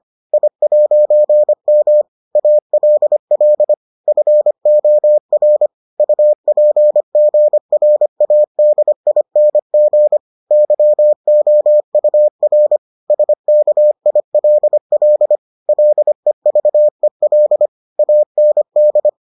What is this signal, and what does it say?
Morse code, heard as a keyed tone, about 25 wpm, 600 Hz. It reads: I'M ALL FOR UPGRADING YOUR SKILL LEVEL AND